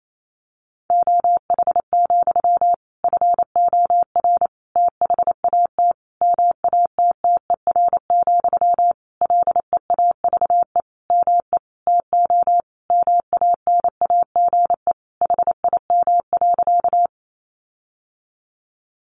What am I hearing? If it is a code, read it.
Morse code, heard as a keyed tone, about 28 wpm, 700 Hz. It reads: OH, FOR THAT MATTER, LEAVE ME TO MANAGE HIM.
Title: Sentences from Top 1400 Words 28wpm